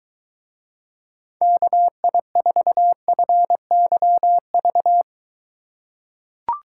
Morse code, heard as a keyed tone, about 23 wpm, 700 Hz. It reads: KI4FYV